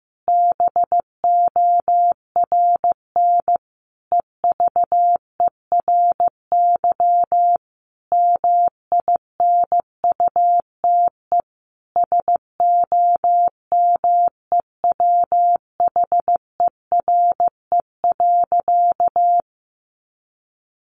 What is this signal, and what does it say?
Morse code, heard as a keyed tone, about 15 wpm, 700 Hz. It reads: BORN EVERY MINUTE SOMEWHERE.
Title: Sentences from Top 1400 Words 15wpm